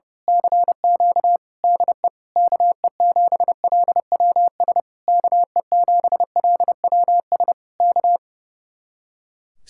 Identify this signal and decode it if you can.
Morse code, heard as a keyed tone, about 28 wpm, 700 Hz. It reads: CQ DE KE7LWS KE7LWS K
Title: Speed-Racing Common QSO Phrases Encore - 30wpm